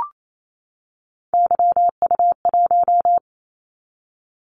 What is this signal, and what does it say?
Morse code, heard as a keyed tone, about 28 wpm, 700 Hz. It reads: YU1
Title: Callsign Prefixes for Top 25 DXCC - 28wpm